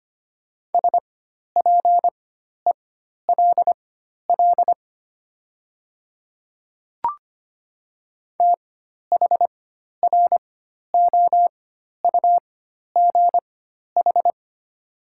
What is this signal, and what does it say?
Morse code, heard as a keyed tone, about 25 wpm, spaced as for 12 wpm, 700 Hz. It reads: SPELL THROUGH
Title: 4x Character Spacing - Top 200 Words - 25wpm